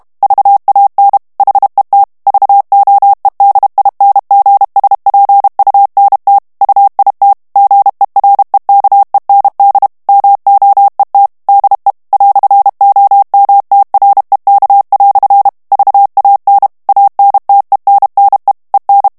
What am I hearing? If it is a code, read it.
Morse code, heard as a keyed tone, about 32 wpm, 800 Hz. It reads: VAN HET VOEDINGSPUNT UIT GEREKEND MOET DE "OMTREK" VAN ANTENNE EN
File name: HOLLANDAIS texte clair 1 vitesse 30 mots minute